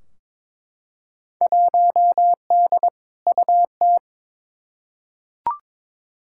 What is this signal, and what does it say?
Morse code, heard as a keyed tone, about 22 wpm, 700 Hz. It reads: 1D UT
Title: Field Day Exchanges - 22wpm